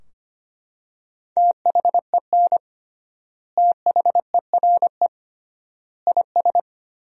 Morse code, heard as a keyed tone, about 25 wpm, 700 Hz. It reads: THEN THERE IS
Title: 3x Word-Spacing - Sentences from Top 100 Words - 25wpm